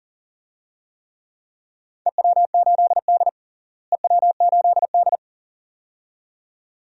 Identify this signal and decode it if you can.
Morse code, heard as a keyed tone, about 40 wpm, 700 Hz. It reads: EW8D EW8D
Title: DXCC-100 Call Signs - 4 Characters - Repeated 2x - 40wpm